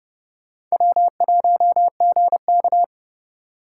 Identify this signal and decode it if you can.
Morse code, heard as a keyed tone, about 35 wpm, 700 Hz. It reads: W1GK
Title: Speed Racing - USA 1x2 Call Signs 30wpm